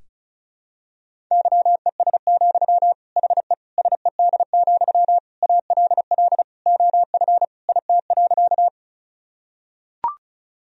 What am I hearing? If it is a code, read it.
Morse code, heard as a keyed tone, about 36 wpm, 700 Hz. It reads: YES, HE SED, ALL OF IT.
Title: Sentences from Top 500 Words and Abbreviations 35wpm